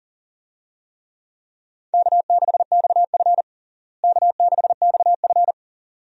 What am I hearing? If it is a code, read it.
Morse code, heard as a keyed tone, about 40 wpm, 700 Hz. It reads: K6XF K6XF
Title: USA Call Signs - 4 Characters - Repeated 2x - 40wpm